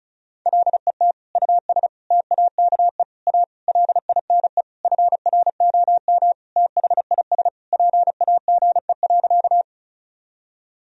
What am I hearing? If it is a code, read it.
Morse code, heard as a keyed tone, about 36 wpm, 700 Hz. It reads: LET US TAKE A LINE FROM THIS PAGE.
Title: Sentences from Top 1000 Words - Review Part 1 - 35wpm